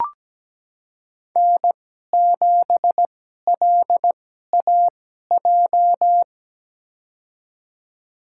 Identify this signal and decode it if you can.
Morse code, heard as a keyed tone, about 17 wpm, spaced as for 12 wpm, 700 Hz. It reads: N7LAJ
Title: USA Call Signs - 5 Characters - 2x Character Spacing - 17wpm